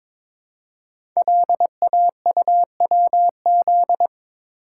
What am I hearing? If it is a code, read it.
Morse code, heard as a keyed tone, about 22 wpm, 690 Hz. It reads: LAUWZ
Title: Code Groups - 5 Letter:Numbers - 22wpm